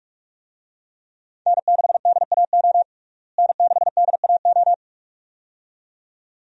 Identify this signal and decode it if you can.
Morse code, heard as a keyed tone, about 42 wpm, 700 Hz. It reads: N6DAO N6DAO
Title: USA Call Signs - 5 Characters - Repeated 2x - 45wpm